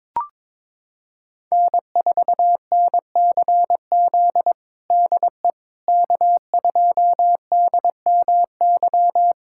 Morse code, heard as a keyed tone, about 22 wpm, 700 Hz. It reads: N4NCZ DE K2DMY
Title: USA Callsign DE Callsign - 5 Characters - 22wpm